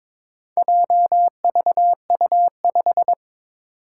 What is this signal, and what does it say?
Morse code, heard as a keyed tone, about 22 wpm, 700 Hz. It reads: JVU5
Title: Four Challenging Letter-Number Combinations 22wpm